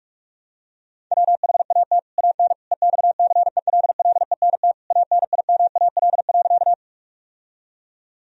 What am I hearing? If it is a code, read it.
Morse code, heard as a keyed tone, about 46 wpm, 700 Hz. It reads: WHAT AN EXCELLENT ANIMAL.
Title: Sentences from Top 1000 Words - Review Part 1 - 45wpm